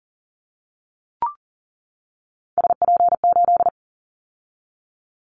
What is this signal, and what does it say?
Morse code, heard as a keyed tone, about 39 wpm, 700 Hz. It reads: SP8
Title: Callsign Prefixes for Top 25 DXCC - 40wpm